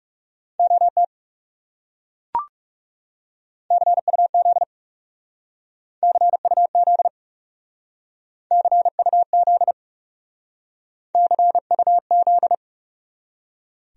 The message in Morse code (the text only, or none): OT CUZ CUZ CUZ CUZ